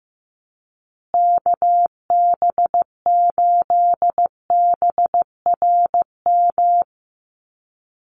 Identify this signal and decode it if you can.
Morse code, heard as a keyed tone, about 15 wpm, 700 Hz. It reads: KB8BRM
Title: USA Call Signs Repeated 2x 15wpm